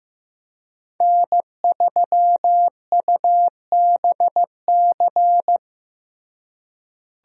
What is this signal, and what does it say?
Morse code, heard as a keyed tone, about 15 wpm, 700 Hz. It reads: N3UBC
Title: USA Call Signs - 5 Characters - Repeated 3x - 15wpm